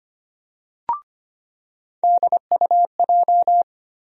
Morse code, heard as a keyed tone, about 25 wpm, 700 Hz. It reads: DUJ
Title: USA Call Sign Postfixes - 3 Letters 25wpm